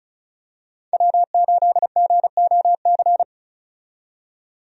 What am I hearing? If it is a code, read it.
Morse code, heard as a keyed tone, about 36 wpm, 700 Hz. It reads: W8GOC
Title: USA Call Signs - 5 Characters - 35wpm